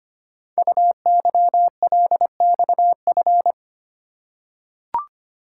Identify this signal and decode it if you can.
Morse code, heard as a keyed tone, about 25 wpm, 700 Hz. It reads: UYLXF
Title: Code Groups - 5 Letters - 25wpm